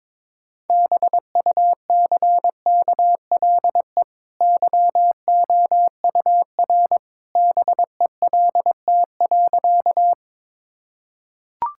Morse code, heard as a keyed tone, about 21 wpm, 690 Hz. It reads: BUCKLE YOUR BELT.
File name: practice / 22wpm / Easy Sentences with Hints - 22wpm